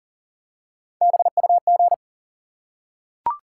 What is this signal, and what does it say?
Morse code, heard as a keyed tone, about 40 wpm, 700 Hz. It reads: BUG
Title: Three Letter Words - 40wpm